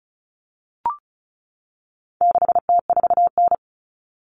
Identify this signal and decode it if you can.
Morse code, heard as a keyed tone, about 34 wpm, 700 Hz. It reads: 6T4N
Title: Common Cut Numbers - 4-Digit 35wpm